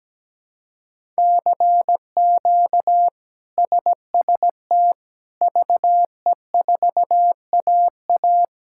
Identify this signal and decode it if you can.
Morse code, heard as a keyed tone, about 17 wpm, 700 Hz. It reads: CQ SST VE4AA
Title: K1USN SST QSO Practice - 17wpm